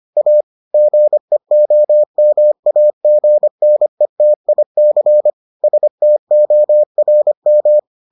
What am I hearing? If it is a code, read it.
Morse code, heard as a keyed tone, about 25 wpm, 600 Hz. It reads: A GEOMAGNETIC STORM